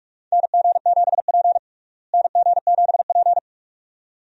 Morse code, heard as a keyed tone, about 45 wpm, 700 Hz. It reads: NG7P NG7P
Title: USA Call Signs - 4 Characters - Repeated 2x - 45wpm